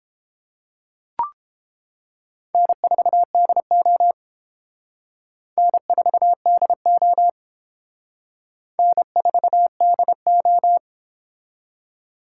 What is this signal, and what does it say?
Morse code, heard as a keyed tone, about 28 wpm, 700 Hz. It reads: N4DO N4DO N4DO